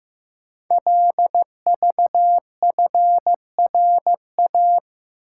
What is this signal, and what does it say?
Morse code, heard as a keyed tone, about 15 wpm, 700 Hz. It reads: LVFRA
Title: Code Groups - 5 Letters - 15wpm